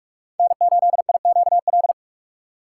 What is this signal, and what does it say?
Morse code, heard as a keyed tone, about 46 wpm, 700 Hz. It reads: N8IQL
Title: USA Call Signs - 5 Characters - 3x Character Spacing - 45wpm